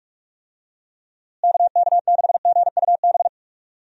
Morse code, heard as a keyed tone, about 46 wpm, 700 Hz. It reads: KK6GUB
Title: USA Call Signs - 6 Characters - Repeated 3x - 45wpm